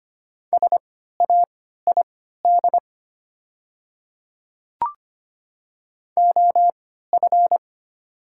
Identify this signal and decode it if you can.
Morse code, heard as a keyed tone, about 25 wpm, spaced as for 14 wpm, 700 Hz. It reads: SAID OF
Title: 3x Character Spacing - Top 100 Words - 25wpm